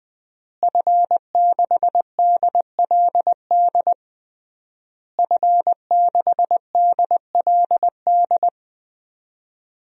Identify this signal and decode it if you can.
Morse code, heard as a keyed tone, about 20 wpm, 700 Hz. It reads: F6DLD F6DLD
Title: DXCC-100 Call Signs - 5 Characters - Repeated 3x - 20wpm